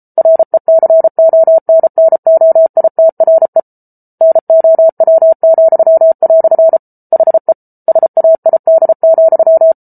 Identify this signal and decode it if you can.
Morse code, heard as a keyed tone, about 34 wpm, 650 Hz. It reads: RECONNOITRE NOW," HE SAID,